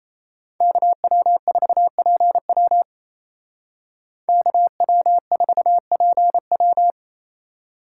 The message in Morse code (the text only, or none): KW4PW KW4PW